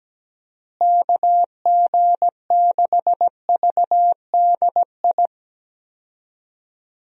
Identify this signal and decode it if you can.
Morse code, heard as a keyed tone, about 17 wpm, 700 Hz. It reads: KG6VDI
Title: USA Call Signs - 6 Characters - Repeated 3x - 17wpm